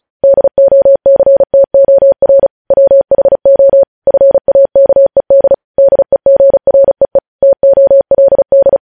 Morse code, heard as a keyed tone, about 36 wpm, 550 Hz. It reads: DOCTOR WHO FAKED DEGREE TOLD
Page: news headlines in morse code at 35 wpm